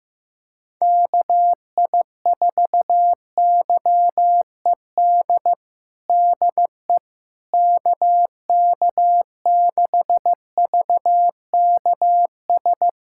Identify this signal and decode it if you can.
Morse code, heard as a keyed tone, about 15 wpm, 700 Hz. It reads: KI4YED DE KK6VKS
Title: USA Callsign DE Callsign - 6 Characters - 15wpm